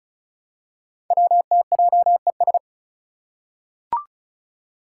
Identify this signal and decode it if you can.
Morse code, heard as a keyed tone, about 36 wpm, 700 Hz. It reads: WTJES
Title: Code Groups - 5 Letters - 35wpm